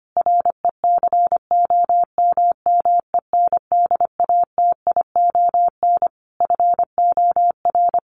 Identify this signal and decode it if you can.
Morse code, heard as a keyed tone, about 25 wpm, 700 Hz. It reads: RECOMMENDATION FOR
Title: Sentences from Top 1400 Words 25wpm